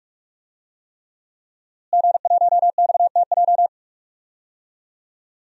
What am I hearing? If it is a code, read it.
Morse code, heard as a keyed tone, about 42 wpm, 700 Hz. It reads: G1XTJ
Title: DXCC-100 Call Signs - 5 Characters - Repeated 3x - 45wpm